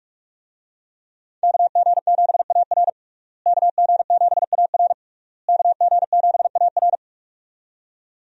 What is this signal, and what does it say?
Morse code, heard as a keyed tone, about 46 wpm, 700 Hz. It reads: KG7AR KG7AR KG7AR
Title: USA Call Signs - 5 Characters - Repeated 3x - 45wpm